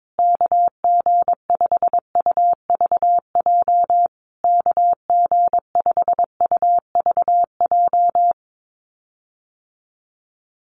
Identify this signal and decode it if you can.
Morse code, heard as a keyed tone, about 22 wpm, 700 Hz. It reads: KG5UVJ KG5UVJ
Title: USA Call Signs Repeated 3x 22wpm